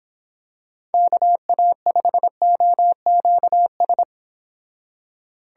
Morse code, heard as a keyed tone, about 27 wpm, 700 Hz. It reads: KA5OQS
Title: Speed Racing - USA 2x3 Call Signs 22wpm